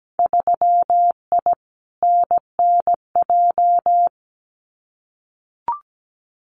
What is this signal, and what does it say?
Morse code, heard as a keyed tone, about 17 wpm, 700 Hz. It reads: 3I NNJ
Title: Winter Field Day Exchanges 17wpm